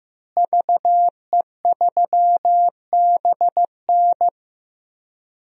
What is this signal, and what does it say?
Morse code, heard as a keyed tone, about 15 wpm, 700 Hz. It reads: VE3BN
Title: DXCC-100 Call Signs - 5 Characters - Repeated 2x - 15wpm